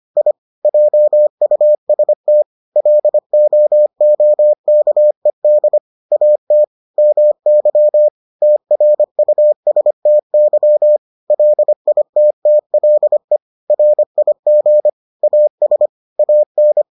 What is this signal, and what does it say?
Morse code, heard as a keyed tone, about 25 wpm, 600 Hz. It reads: I JUST LOOKED AT MY TRUSTY LITTLE RIG AS AN